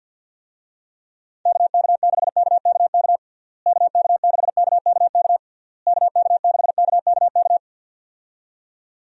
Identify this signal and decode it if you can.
Morse code, heard as a keyed tone, about 49 wpm, 700 Hz. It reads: KK6KKK KK6KKK KK6KKK
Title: USA Call Signs - 6 Characters - Repeated 3x - 50wpm